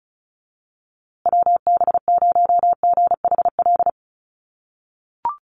Morse code, heard as a keyed tone, about 34 wpm, 700 Hz. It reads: WB0GHL
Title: USA Call Signs Repeated 2x 35wpm